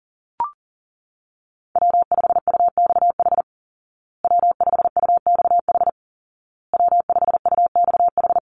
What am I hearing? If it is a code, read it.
Morse code, heard as a keyed tone, about 40 wpm, 700 Hz. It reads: W5UXH W5UXH W5UXH
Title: Call Signs Repeated 2x - 40wpm